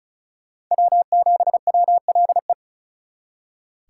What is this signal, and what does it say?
Morse code, heard as a keyed tone, about 46 wpm, 700 Hz. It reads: W7WLE